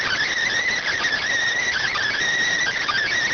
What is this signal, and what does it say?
Emergency Alert System (SAME) tones, heard as harsh buzzing bursts. It is heard from 0.5 to 3.0 s.